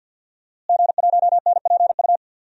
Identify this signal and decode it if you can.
Morse code, heard as a keyed tone, about 48 wpm, 700 Hz. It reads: G1NPU
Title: DXCC-100 Call Signs - 5 Characters - Repeated 2x - 50wpm